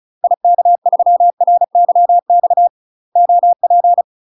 Morse code, heard as a keyed tone, about 34 wpm, 700 Hz. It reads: IK3RYX OP
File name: UK MASTER QSOs TRAINING / 35WPM / UK MASTER QSOs 0018